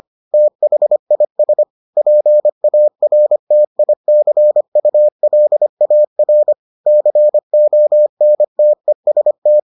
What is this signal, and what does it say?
Morse code, heard as a keyed tone, about 25 wpm, 600 Hz. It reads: THIS PARTICULAR CONTEST